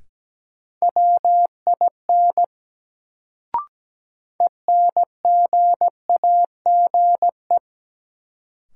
Morse code, heard as a keyed tone, about 17 wpm, 700 Hz. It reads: WIN ENGAGE
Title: Sentences from Top 1500 Words - New Word Drill Alt - 17wpm